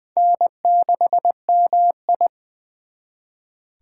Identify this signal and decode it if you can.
Morse code, heard as a keyed tone, about 24 wpm, 700 Hz. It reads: N6MI
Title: Speed Racing - USA 1x2 Call Signs 20wpm